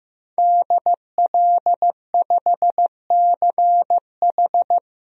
Morse code, heard as a keyed tone, about 15 wpm, 700 Hz. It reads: DL5CH
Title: DXCC-100 Call Signs - 5 Characters - Repeated 3x - 15wpm